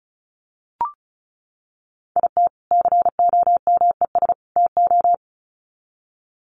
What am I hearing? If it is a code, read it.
Morse code, heard as a keyed tone, about 34 wpm, 700 Hz. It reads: IT COMES TO
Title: Sets of 3 Words 35wpm